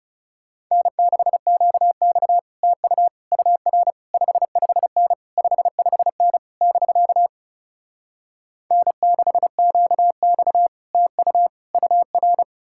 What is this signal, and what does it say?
Morse code, heard as a keyed tone, about 40 wpm, 700 Hz. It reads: N6QX TU UR 55N 55N <BK> N6QX TU UR